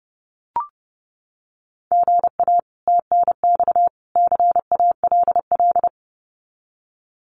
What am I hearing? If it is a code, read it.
Morse code, heard as a keyed tone, about 30 wpm, 700 Hz. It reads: GA TNX CALL
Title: Common QSO Phrases 30wpm